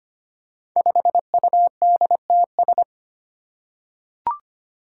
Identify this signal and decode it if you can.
Morse code, heard as a keyed tone, about 25 wpm, 700 Hz. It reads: 5UDTS
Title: Five Challenging Letter-Number Combinations 25wpm